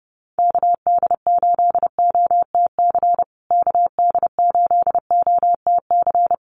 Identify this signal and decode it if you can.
Morse code, heard as a keyed tone, about 30 wpm, 700 Hz. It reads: KD8OTC KD8OTC
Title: USA Call Signs Repeated 3x 30wpm